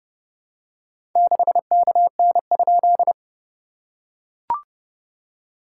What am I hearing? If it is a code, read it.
Morse code, heard as a keyed tone, about 30 wpm, 700 Hz. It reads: 6KN?
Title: SOTA QSO Practice 30wpm